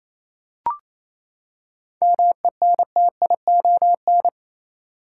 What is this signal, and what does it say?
Morse code, heard as a keyed tone, about 28 wpm, 700 Hz. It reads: MENTION
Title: Seven Letter Words - 28wpm